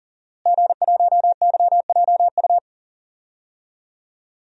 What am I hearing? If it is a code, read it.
Morse code, heard as a keyed tone, about 40 wpm, 700 Hz. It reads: G1YJU
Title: DXCC-100 Call Signs - 5 Characters - Repeated 3x - 40wpm